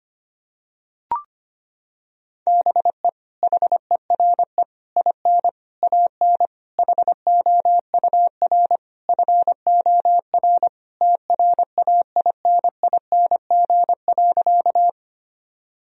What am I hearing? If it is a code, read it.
Morse code, heard as a keyed tone, about 25 wpm, 700 Hz. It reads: BE HERE IN AN HOUR FOR TRAINING.
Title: Sentences from Top 1000 Words - Review Part 4 - 25wpm